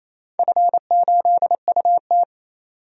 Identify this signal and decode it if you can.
Morse code, heard as a keyed tone, about 28 wpm, 700 Hz. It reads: F8UT
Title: DXCC-100 Call Signs - 4 Characters - 28wpm